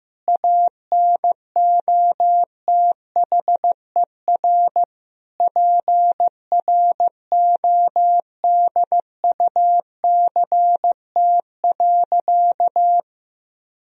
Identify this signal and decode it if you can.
Morse code, heard as a keyed tone, about 15 wpm, 700 Hz. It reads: ANOTHER PRODUCT.
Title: Sentences from Top 1000 Words - Review Part 3 - 15wpm